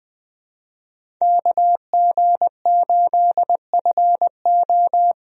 Morse code, heard as a keyed tone, about 20 wpm, 700 Hz. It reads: KG8FO
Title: USA Call Signs - 5 Characters - 20wpm